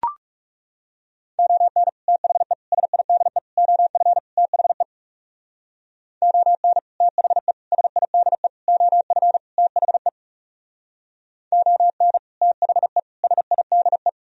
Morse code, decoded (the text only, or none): ON THE SIDE OF THE ON THE SIDE OF THE ON THE SIDE